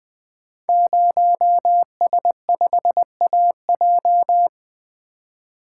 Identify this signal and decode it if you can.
Morse code, heard as a keyed tone, about 20 wpm, 700 Hz. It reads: 0S5AJ
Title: Code Groups - 5 Letter:Numbers - 20wpm